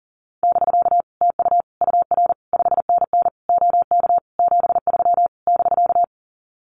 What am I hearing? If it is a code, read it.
Morse code, heard as a keyed tone, about 40 wpm, 700 Hz. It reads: <BK> TU UR 5NN OK 73 <BK>
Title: Common QSO Phrases Encore 40wpm